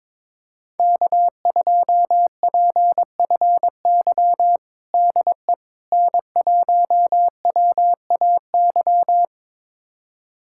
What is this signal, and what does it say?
Morse code, heard as a keyed tone, about 22 wpm, 700 Hz. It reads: K2PFY DE N1WAY
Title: USA Callsign DE Callsign 22wpm